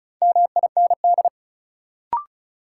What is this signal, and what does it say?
Morse code, heard as a keyed tone, about 36 wpm, 700 Hz. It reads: MIND
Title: Sets of 5 Words - New Word Drill - 35wpm